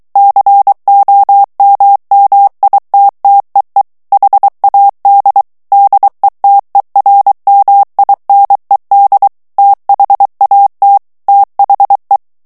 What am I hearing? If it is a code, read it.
Morse code, heard as a keyed tone, about 23 wpm, 800 Hz. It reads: COMMITTEE HAD DETERMINED THAT THE
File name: ANGLAIS texte clair 3 vitesse 22 mots minute